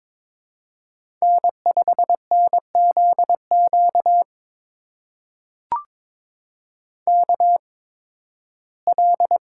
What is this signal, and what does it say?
Morse code, heard as a keyed tone, about 22 wpm, spaced as for 15 wpm, 700 Hz. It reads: N5NZQ KL
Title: USA Call Signs - 5 Characters - 8x Character Spacing - 22wpm